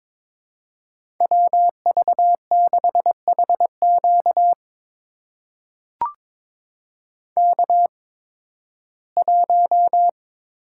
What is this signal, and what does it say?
Morse code, heard as a keyed tone, about 22 wpm, spaced as for 14 wpm, 700 Hz. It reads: WV6HQ K1
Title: USA Call Signs - 5 Characters - 8x Character Spacing - 22wpm